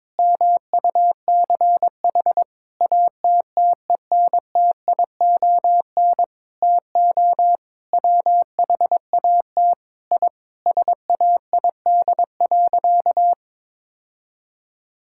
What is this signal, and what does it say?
Morse code, heard as a keyed tone, about 22 wpm, 700 Hz. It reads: MUCH ATTENTION TO WHAT I SAID.
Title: Sentences from Top 1000 Words - Review Part 6 - 22wpm